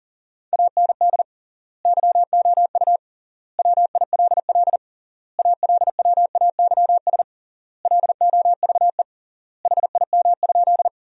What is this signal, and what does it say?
Morse code, heard as a keyed tone, about 40 wpm, 700 Hz. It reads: AND YOU WILL ALWAYS LOVE HIM?
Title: 3x Word-Spacing - Sentences from Top 400 Words - 40wpm